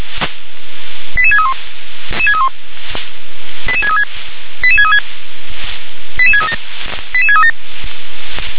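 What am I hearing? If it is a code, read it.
Five-tone selective call (ZVEI), heard as quick runs of five tones, tones of about 70 ms. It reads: 88631 88636 88636 88637 88637
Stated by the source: SDR-received ZVEI five-tone call, code 88511.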